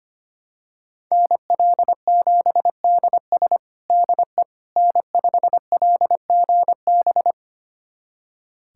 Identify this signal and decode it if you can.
Morse code, heard as a keyed tone, about 25 wpm, 700 Hz. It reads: NL7DS DE N5LGB
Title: USA Callsign DE Callsign - 5 Characters - 25wpm